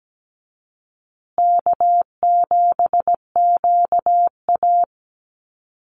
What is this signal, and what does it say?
Morse code, heard as a keyed tone, about 17 wpm, 700 Hz. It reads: K7QA
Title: Easy Call Signs Repeated 2x - 17wpm